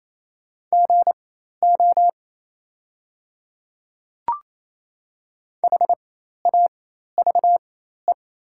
Morse code, heard as a keyed tone, about 28 wpm, spaced as for 13 wpm, 690 Hz. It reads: GO HAVE
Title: 4x Character Spacing - Top 200 Words - 28wpm